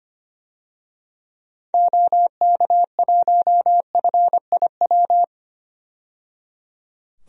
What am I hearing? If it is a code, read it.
Morse code, heard as a keyed tone, about 25 wpm, 700 Hz. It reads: OK1FIW
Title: DXCC-100 Call Signs - 6 Characters - 25wpm